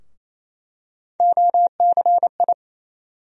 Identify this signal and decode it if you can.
Morse code, heard as a keyed tone, about 28 wpm, 700 Hz. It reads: OCI
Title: USA Call Sign Postfixes - 3 Letters 28wpm